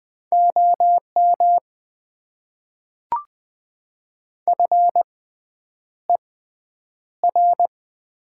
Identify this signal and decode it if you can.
Morse code, heard as a keyed tone, about 20 wpm, spaced as for 9 wpm, 700 Hz. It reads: OM FER
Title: Top 90 QSO Elements - 6x Character Spacing - 20wpm